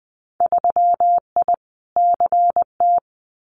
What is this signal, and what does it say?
Morse code, heard as a keyed tone, about 20 wpm, 700 Hz. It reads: 3I CT
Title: Winter Field Day Exchanges 20wpm